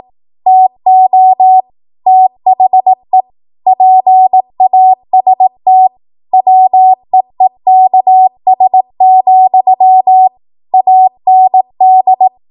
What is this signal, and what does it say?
Morse code, heard as a keyed tone, about 18 wpm, 750 Hz. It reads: TO THE PAST WEEKS, AND
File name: Morse / 18WPM